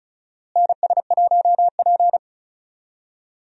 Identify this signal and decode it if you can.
Morse code, heard as a keyed tone, about 34 wpm, 700 Hz. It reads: NS1P
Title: USA Call Signs - 4 Characters - Repeated 3x - 35wpm